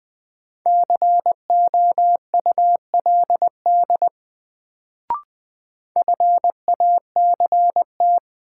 Morse code, heard as a keyed tone, about 20 wpm, 700 Hz. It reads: COULD FACT
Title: Sets of 2 Words - New Word Drill - 20wpm